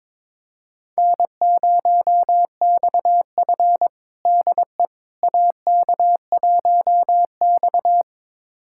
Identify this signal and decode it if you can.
Morse code, heard as a keyed tone, about 22 wpm, 700 Hz. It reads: N0XF DE AK1X
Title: USA Callsign DE Callsign - 4 Characters - 22wpm